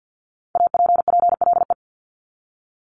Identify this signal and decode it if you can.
Morse code, heard as a keyed tone, about 50 wpm, 700 Hz. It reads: APPLE
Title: Multi-letter Phonemes Repeated 3x 50wpm